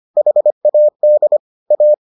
Morse code, heard as a keyed tone, about 25 wpm, 600 Hz. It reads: HAD A